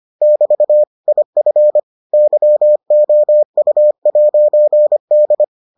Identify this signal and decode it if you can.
Morse code, heard as a keyed tone, about 25 wpm, 600 Hz. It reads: <BT> IF YOU'D